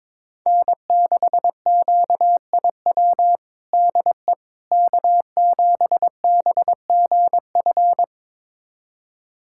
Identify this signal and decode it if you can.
Morse code, heard as a keyed tone, about 22 wpm, 700 Hz. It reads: N6QIW DE K7BGF
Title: USA Callsign DE Callsign - 5 Characters - 22wpm